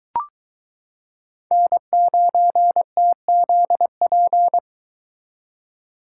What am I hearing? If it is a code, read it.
Morse code, heard as a keyed tone, about 23 wpm, 700 Hz. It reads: N9TZP